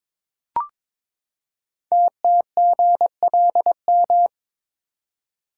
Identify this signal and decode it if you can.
Morse code, heard as a keyed tone, about 22 wpm, 700 Hz. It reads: TTGLM
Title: Code Groups - 5 Letters - 22wpm